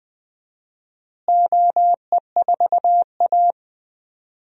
Morse code, heard as a keyed tone, about 20 wpm, 700 Hz. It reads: OE4A
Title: DXCC-100 Call Signs - 4 Characters - Repeated 2x - 20wpm